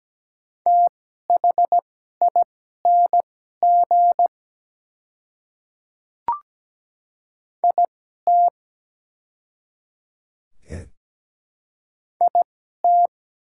Morse code, heard as a keyed tone, about 17 wpm, spaced as for 12 wpm, 700 Hz. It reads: THING IT IT
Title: 2x Character Spacing - Top 100 Words - 17wpm